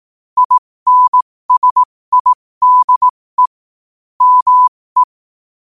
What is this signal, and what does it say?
Morse code, heard as a keyed tone, about 18 wpm, spaced as for 15 wpm, 1000 Hz. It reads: INSIDE ME
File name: qotdfiles / 15wpm